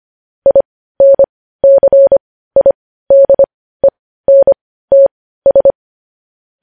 Morse code, heard as a keyed tone, about 25 wpm, spaced as for 15 wpm, 550 Hz. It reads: INCIDENTS
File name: podcast / News headlines-1x-25x15